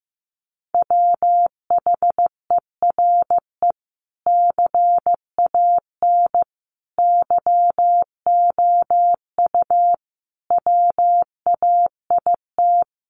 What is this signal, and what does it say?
Morse code, heard as a keyed tone, about 15 wpm, 700 Hz. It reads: WHERE CAN YOU WAIT
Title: Sentences from Top 500 Words 15wpm